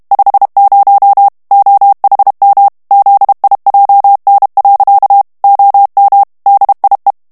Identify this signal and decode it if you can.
Morse code, heard as a keyed tone, about 32 wpm, 800 Hz. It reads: 50 OHM ZIJN. OM DIE
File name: HOLLANDAIS texte clair 1 vitesse 30 mots minute